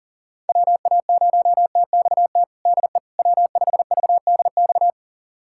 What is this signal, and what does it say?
Morse code, heard as a keyed tone, about 40 wpm, 700 Hz. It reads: WA0TXT DE W5VDX
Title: USA Callsign DE Callsign 40wpm